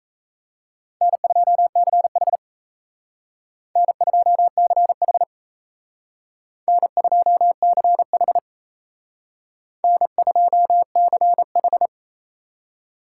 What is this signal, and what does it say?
Morse code, heard as a keyed tone, about 35 wpm, 700 Hz. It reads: N2CH N2CH N2CH N2CH